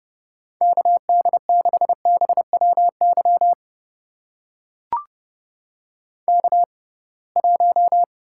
Morse code, heard as a keyed tone, about 30 wpm, spaced as for 23 wpm, 700 Hz. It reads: KD6BWY K1
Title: USA Call Signs - 6x Character Spacing - 30wpm